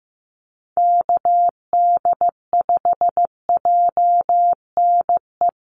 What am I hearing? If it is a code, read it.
Morse code, heard as a keyed tone, about 15 wpm, 700 Hz. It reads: KD5JNE
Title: USA Call Signs Repeated 3x 15wpm